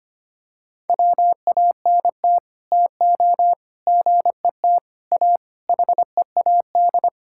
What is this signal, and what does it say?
Morse code, heard as a keyed tone, about 25 wpm, 700 Hz. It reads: WANT TO GET A HEAD
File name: Sets of 9 Words - 25wpm